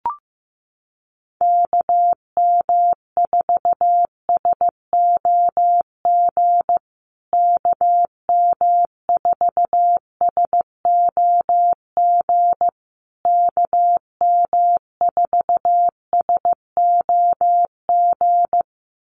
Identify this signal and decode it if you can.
Morse code, heard as a keyed tone, about 15 wpm, 700 Hz. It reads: KM4SOG KM4SOG KM4SOG
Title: USA Call Signs Repeated 3x 15wpm